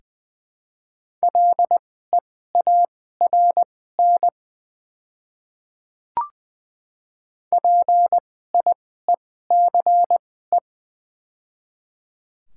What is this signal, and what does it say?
Morse code, heard as a keyed tone, about 20 wpm, spaced as for 14 wpm, 700 Hz. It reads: LEARN PIECE
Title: 2x Character Spacing - Top 500 Words - 20wpm